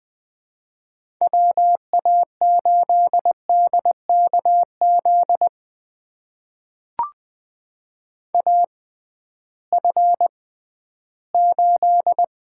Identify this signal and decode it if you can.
Morse code, heard as a keyed tone, about 20 wpm, spaced as for 13 wpm, 700 Hz. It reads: WA8DKZ AF8
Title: USA Call Signs - 6x Character Spacing - 20wpm